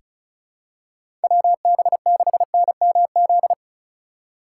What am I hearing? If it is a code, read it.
Morse code, heard as a keyed tone, about 36 wpm, 700 Hz. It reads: WB6NMZ
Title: USA Call Signs - 6 Characters - 3x Character Spacing - 35wpm